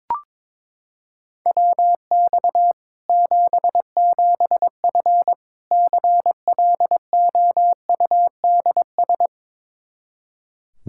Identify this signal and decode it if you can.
Morse code, heard as a keyed tone, about 21 wpm, 700 Hz. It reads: WX 77F CLOUDS
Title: Practice QSO - Exchange 2 - Info - 22wpm